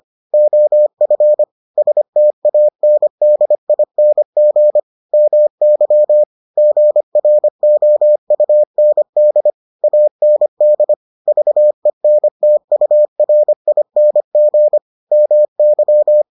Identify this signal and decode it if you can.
Morse code, heard as a keyed tone, about 25 wpm, 600 Hz. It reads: OF STANDING MY GROUND AND VENTURING MY